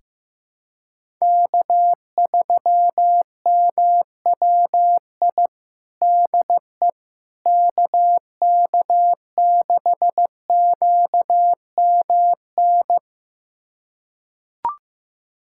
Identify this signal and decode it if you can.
Morse code, heard as a keyed tone, about 15 wpm, 700 Hz. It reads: K3MWI DE KK6QMN
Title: USA Callsign DE Callsign 15wpm